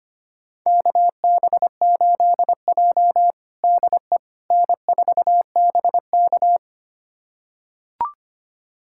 Morse code, heard as a keyed tone, about 25 wpm, 700 Hz. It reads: KB8J DE N4BK
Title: USA Callsign DE Callsign - 4 Characters - 25wpm